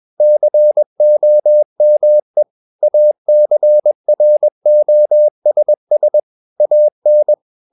Morse code, heard as a keyed tone, about 21 wpm, 600 Hz. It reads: COME ACROSS AN
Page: CW med Gnister